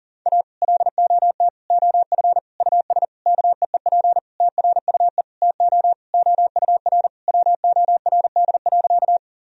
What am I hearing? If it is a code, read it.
Morse code, heard as a keyed tone, about 37 wpm, 700 Hz. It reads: A LOT OF US KEEP TRUE TO OUR WORD.
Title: Sentences from Top 1000 Words - Review Part 3 - 40wpm